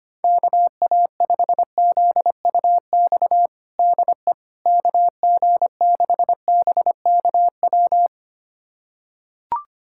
Morse code, heard as a keyed tone, about 25 wpm, 700 Hz. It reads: KA5ZUX DE KG6BKW
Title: USA Callsign DE Callsign 25wpm